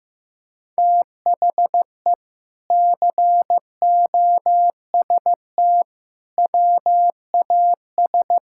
Morse code, heard as a keyed tone, about 15 wpm, 700 Hz. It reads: THE COST WAS
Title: Sentences from Top 1000 Words - Review Part 6 - 15wpm